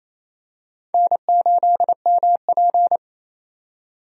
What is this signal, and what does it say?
Morse code, heard as a keyed tone, about 27 wpm, 700 Hz. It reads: N8MP
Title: USA Call Signs - 4 Characters - 28wpm